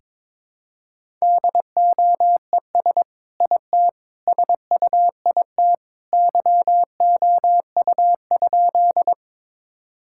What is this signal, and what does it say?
Morse code, heard as a keyed tone, about 22 wpm, 700 Hz. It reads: DOES IT SUIT YOU?
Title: Sentences from Top 1600 Words 22wpm